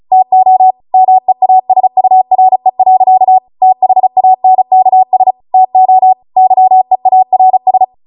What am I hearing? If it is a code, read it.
Morse code, heard as a keyed tone, about 36 wpm, 750 Hz. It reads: TO MEASURE. THANKS TO YEARS